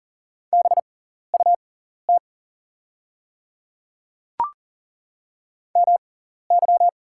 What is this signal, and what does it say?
Morse code, heard as a keyed tone, about 40 wpm, spaced as for 14 wpm, 700 Hz. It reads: BUT MY
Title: 6x Character Spacing - Top 200 Words - 40wpm